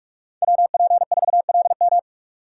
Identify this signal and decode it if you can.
Morse code, heard as a keyed tone, about 47 wpm, 700 Hz. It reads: WP4LM
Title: USA Call Signs - 4x Character Spacing - 45wpm